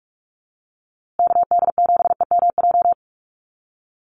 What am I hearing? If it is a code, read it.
Morse code, heard as a keyed tone, about 44 wpm, 700 Hz. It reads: KD7EMJ
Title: USA Call Signs Repeated 3x 45wpm